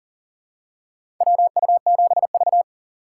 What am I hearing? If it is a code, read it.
Morse code, heard as a keyed tone, about 40 wpm, 700 Hz. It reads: WU7V
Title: USA Call Signs - 8x Character Spacing - 40wpm